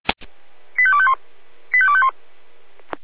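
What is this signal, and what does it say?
Five-tone selective call (ZVEI), heard as quick runs of five tones, tones of about 70 ms. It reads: 85251 85251
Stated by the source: Five-tone ZVEI selective call received by SDR, from the feuerwehr-orlamuende.org fire brigade audio server.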